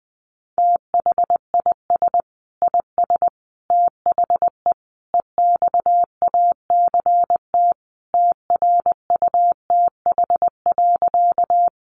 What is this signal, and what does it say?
Morse code, heard as a keyed tone, about 20 wpm, 700 Hz. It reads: THIS IS THE EXACT TRUTH.
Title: Sentences from Top 1300 Words 20wpm